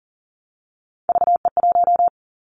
Morse code, heard as a keyed tone, about 40 wpm, 700 Hz. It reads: VE1
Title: Callsign Prefixes for Top 50 DXCC - 40wpm